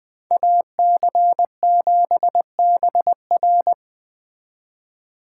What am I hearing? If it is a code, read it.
Morse code, heard as a keyed tone, about 21 wpm, 700 Hz. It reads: AC7BR